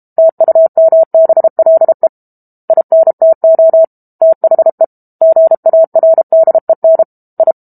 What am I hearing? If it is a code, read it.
Morse code, heard as a keyed tone, about 32 wpm, 650 Hz. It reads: TUMBLE INTO THE GARDEN I
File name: Book Two / 06-Chapter 6 - 32 WPM